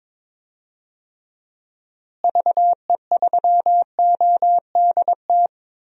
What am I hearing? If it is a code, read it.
Morse code, heard as a keyed tone, about 22 wpm, 700 Hz. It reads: VE3ODT
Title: DXCC-100 Call Signs - 6 Characters - Repeated 3x - 22wpm